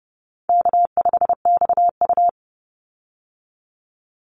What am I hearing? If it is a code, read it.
Morse code, heard as a keyed tone, about 30 wpm, 700 Hz. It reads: K5XU
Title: Call Signs Repeated 3x - 30wpm